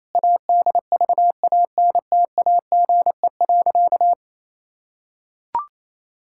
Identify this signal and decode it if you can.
Morse code, heard as a keyed tone, about 28 wpm, 700 Hz. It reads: ADVANTAGE.
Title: Sentences from Top 1000 Words - Review Part 6 - 28wpm